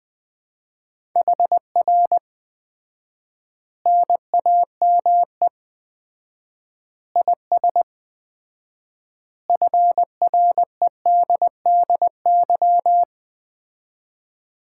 Morse code, heard as a keyed tone, about 20 wpm, 700 Hz. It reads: HR NAME IS FREDDY
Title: Common QSO Phrases - 4x Word-Spacing - 20wpm